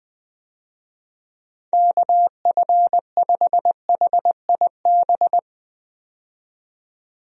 Morse code, heard as a keyed tone, about 20 wpm, 700 Hz. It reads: KF5HIB
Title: USA Call Signs - 6 Characters - 20wpm